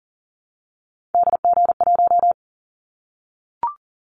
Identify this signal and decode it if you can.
Morse code, heard as a keyed tone, about 42 wpm, 700 Hz. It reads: DG1
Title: Callsign Prefixes for Top 25 DXCC - 40wpm